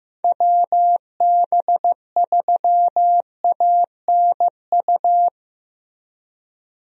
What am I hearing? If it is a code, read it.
Morse code, heard as a keyed tone, about 15 wpm, 700 Hz. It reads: WB3ANU
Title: USA Call Signs - 6 Characters - Repeated 2x - 15wpm